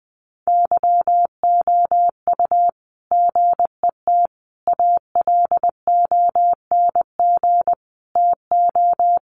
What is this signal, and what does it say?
Morse code, heard as a keyed tone, about 20 wpm, 700 Hz. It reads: YOU GET ALONG TO
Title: Sentences from Top 700 Words 20wpm